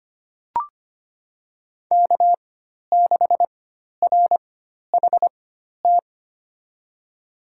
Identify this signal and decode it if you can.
Morse code, heard as a keyed tone, about 25 wpm, spaced as for 12 wpm, 700 Hz. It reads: K6RHT
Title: USA Call Signs - 5 Characters - 4x Character Spacing - 25wpm